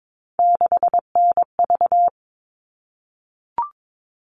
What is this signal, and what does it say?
Morse code, heard as a keyed tone, about 22 wpm, 700 Hz. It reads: 6NV
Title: All Cut Numbers - 3-Digit 22wpm